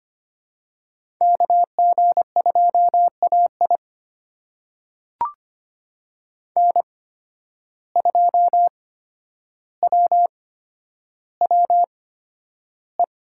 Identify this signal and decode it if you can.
Morse code, heard as a keyed tone, about 25 wpm, spaced as for 11 wpm, 700 Hz. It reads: KG2AI N2WWE
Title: USA Call Signs - 5 Characters - 8x Character Spacing - 25wpm